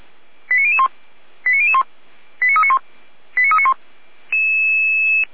Five-tone selective call (ZVEI), heard as quick runs of five tones, tones of about 70 ms. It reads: 89001 89001 89391 89391